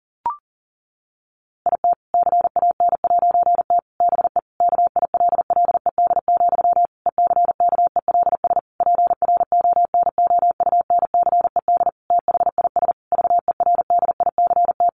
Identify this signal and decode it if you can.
Morse code, heard as a keyed tone, about 40 wpm, 700 Hz. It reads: IT CAN'T BE KILLED, ECKELS PRONOUNCED THIS VERDICT
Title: A Sound Of Thunder 40wpm